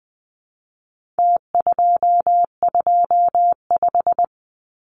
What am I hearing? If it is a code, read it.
Morse code, heard as a keyed tone, about 20 wpm, 700 Hz. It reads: T225
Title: Numbers - Contest Style - 4-Digit 20wpm